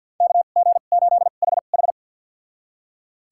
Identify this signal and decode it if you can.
Morse code, heard as a keyed tone, about 48 wpm, spaced as for 35 wpm, 700 Hz. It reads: KG8HH
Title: USA Call Signs - 5 Characters - 2x Character Spacing - 50wpm